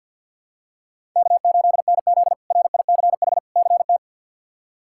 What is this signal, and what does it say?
Morse code, heard as a keyed tone, about 46 wpm, 700 Hz. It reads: K8NZ RICH CT
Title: K1USN SST QSO Practice - 50wpm